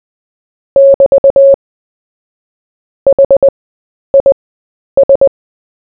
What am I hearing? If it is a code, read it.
Morse code, heard as a keyed tone, about 20 wpm, spaced as for 10 wpm, 550 Hz. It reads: <BT> HIS
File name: cw / fists / Short-Story-4-10wpm